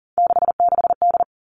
Morse code, heard as a keyed tone, about 40 wpm, 700 Hz. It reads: 66D
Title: All Cut Numbers - 3-Digit 40wpm